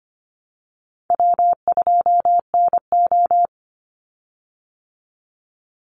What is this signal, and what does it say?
Morse code, heard as a keyed tone, about 25 wpm, 700 Hz. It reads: W2NO
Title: Call Signs Repeated 3x - 25wpm